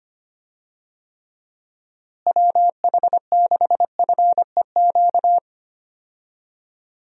Morse code, heard as a keyed tone, about 25 wpm, 700 Hz. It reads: WH6FEQ
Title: USA Call Signs - 6 Characters - Repeated 2x - 25wpm